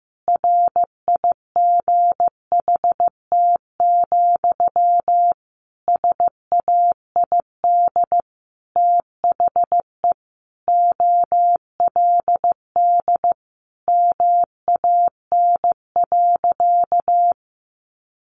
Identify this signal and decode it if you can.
Morse code, heard as a keyed tone, about 15 wpm, 700 Hz. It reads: RIGHT, SAID THE OLD MAN.
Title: Sentences from Top 200 Words 15wpm